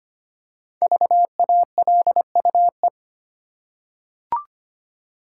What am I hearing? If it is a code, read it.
Morse code, heard as a keyed tone, about 25 wpm, 700 Hz. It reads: VALUE
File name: 5 Letter Words - First 3 Letters Given - 25wpm